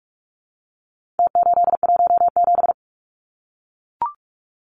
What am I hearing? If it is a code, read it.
Morse code, heard as a keyed tone, about 44 wpm, 700 Hz. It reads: T817
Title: Numbers - Contest Style - 4-Digit 45wpm